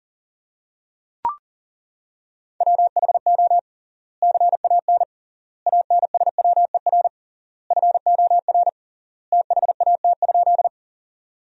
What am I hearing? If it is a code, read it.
Morse code, heard as a keyed tone, about 40 wpm, 700 Hz. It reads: WHO CAN ANSWER FOR THAT?
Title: 3x Word-Spacing - Sentences from Top 300 Words - 40wpm